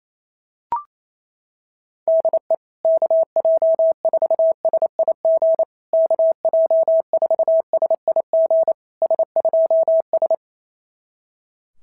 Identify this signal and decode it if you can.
Morse code, heard as a keyed tone, about 29 wpm, 650 Hz. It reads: DE KJ4SIG KJ4SIG S2S
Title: SOTA QSO Practice 28wpm